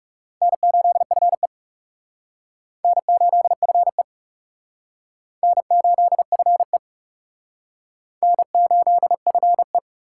Morse code, heard as a keyed tone, about 36 wpm, 700 Hz. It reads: N8FE N8FE N8FE N8FE
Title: Speed Racing - USA 1x2 Call Signs 30wpm